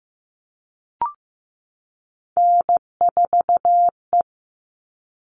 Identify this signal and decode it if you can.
Morse code, heard as a keyed tone, about 15 wpm, 700 Hz. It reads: N4E
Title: All Cut Numbers - 3-Digit 15wpm